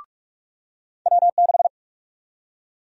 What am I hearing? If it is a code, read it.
Morse code, heard as a keyed tone, about 46 wpm, 700 Hz. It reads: W6
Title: USA Call Sign Prefixes - 45wpm